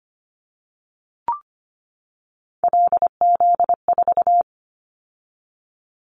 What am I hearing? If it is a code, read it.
Morse code, heard as a keyed tone, about 25 wpm, 700 Hz. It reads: LZ4
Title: Callsign Prefixes for Top 50 DXCC - 25wpm